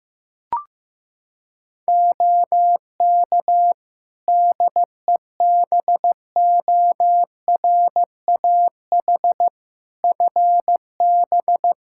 Morse code, heard as a keyed tone, about 15 wpm, 700 Hz. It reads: OK DEBORAH FB
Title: Practice QSO - Exchange 2 - Info - 15wpm